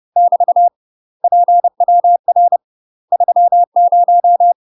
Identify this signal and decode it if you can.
Morse code, heard as a keyed tone, about 30 wpm, 700 Hz. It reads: <BT> PWR 30
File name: UK MASTER QSOs TRAINING / 30WPM / UK MASTER QSOs 0021